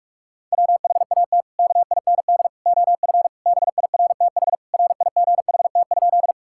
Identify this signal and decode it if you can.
Morse code, heard as a keyed tone, about 41 wpm, 700 Hz. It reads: WHAT KIND OF BIRTHRIGHT?
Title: Sentences from Top 1000 Words - Review Part 5 - 45wpm